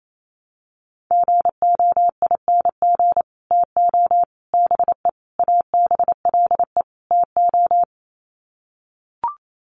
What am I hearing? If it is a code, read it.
Morse code, heard as a keyed tone, about 28 wpm, 700 Hz. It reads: GOING TO BE ABLE TO
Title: Sets of 5 Words 28wpm